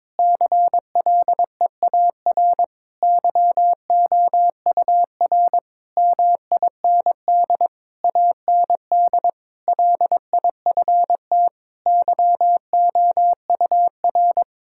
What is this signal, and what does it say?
Morse code, heard as a keyed tone, about 22 wpm, 700 Hz. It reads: CLEAR YOUR MIND AND LIFT YOUR
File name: Difficult Sentences with Hints - 22wpm